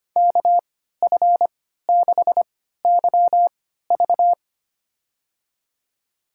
Morse code, heard as a keyed tone, about 25 wpm, spaced as for 14 wpm, 700 Hz. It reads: KF6YV
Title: USA Call Signs - 5 Characters - 3x Character Spacing - 25wpm